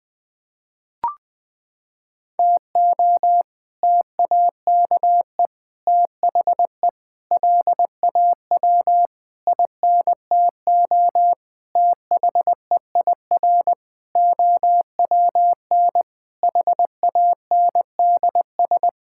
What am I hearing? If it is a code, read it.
Morse code, heard as a keyed tone, about 20 wpm, 700 Hz. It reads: TO TAKE THE LAW INTO THEIR OWN HANDS
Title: Sets of 8 Words - 20wpm